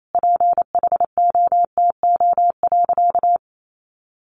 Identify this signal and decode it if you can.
Morse code, heard as a keyed tone, about 28 wpm, 700 Hz. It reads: PHOTO.
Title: Sentences from Top 1200 Words 28wpm